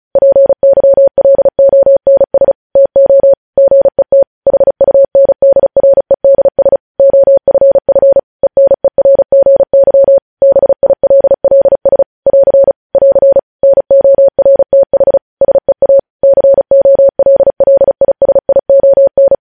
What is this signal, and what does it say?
Morse code, heard as a keyed tone, about 36 wpm, 550 Hz. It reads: PYLONS TO GET HUNDREDS OFF ENERGY BILLS <AR> <AR> NORTH SEA COLLISION
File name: podcast / News headlines-1x-35x35